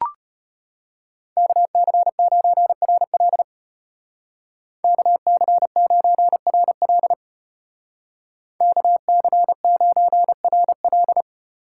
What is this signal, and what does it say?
Morse code, heard as a keyed tone, about 33 wpm, 700 Hz. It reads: KC9RL KC9RL KC9RL